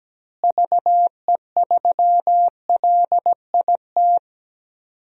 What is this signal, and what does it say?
Morse code, heard as a keyed tone, about 17 wpm, 700 Hz. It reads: VE3LIT
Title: DXCC-100 Call Signs - 6 Characters - Repeated 3x - 17wpm